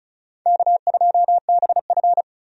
Morse code, heard as a keyed tone, about 34 wpm, 700 Hz. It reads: K2BF
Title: USA Call Signs - 4 Characters - 35wpm